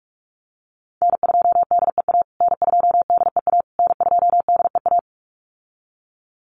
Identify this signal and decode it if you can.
Morse code, heard as a keyed tone, about 44 wpm, 700 Hz. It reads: N2DEA N2DEA N2DEA
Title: USA Call Signs Repeated 3x 45wpm